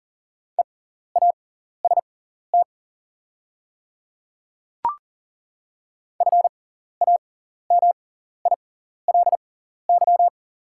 Morse code, heard as a keyed tone, about 40 wpm, spaced as for 14 wpm, 690 Hz. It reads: EAST FAMILY
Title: 6x Character Spacing - Top 500 Words - 40wpm